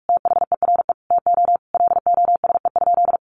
Morse code, heard as a keyed tone, about 44 wpm, 700 Hz. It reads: THERE TO LOSE?
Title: Sentences from Top 1100 Words 45wpm